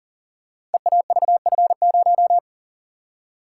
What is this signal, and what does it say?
Morse code, heard as a keyed tone, about 38 wpm, 700 Hz. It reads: EAVF0
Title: Code Groups - 5 Letter:Numbers - 40wpm